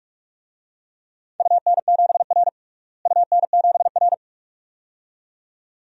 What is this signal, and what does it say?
Morse code, heard as a keyed tone, about 47 wpm, 700 Hz. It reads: UN7R UN7R
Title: DXCC-100 Call Signs - 4 Characters - Repeated 2x - 45wpm